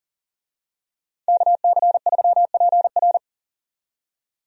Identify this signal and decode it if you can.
Morse code, heard as a keyed tone, about 40 wpm, 700 Hz. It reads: KC3PR
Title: USA Call Signs - 5 Characters - 40wpm